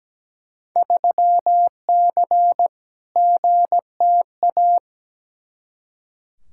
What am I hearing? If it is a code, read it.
Morse code, heard as a keyed tone, about 17 wpm, 700 Hz. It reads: 3C GTA
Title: Field Day Exchanges Encore - 17wpm